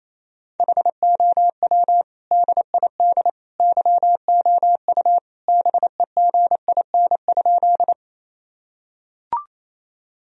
Morse code, heard as a keyed tone, about 28 wpm, 700 Hz. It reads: HOW DID YOU BEGIN?
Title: Sentences from Top 1000 Words - Review Part 1 - 28wpm